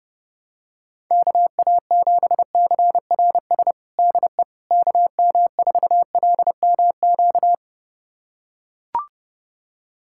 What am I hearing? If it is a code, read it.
Morse code, heard as a keyed tone, about 31 wpm, 700 Hz. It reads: KA7CRS DE KM4LMQ
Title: USA Callsign DE Callsign 30wpm